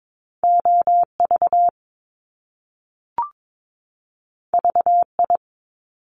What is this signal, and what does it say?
Morse code, heard as a keyed tone, about 22 wpm, 700 Hz. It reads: OV VI
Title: Top Two Letter Combinations in Words 22wpm